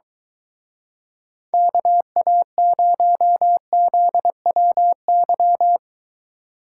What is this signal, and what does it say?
Morse code, heard as a keyed tone, about 23 wpm, 700 Hz. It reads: KA0ZWY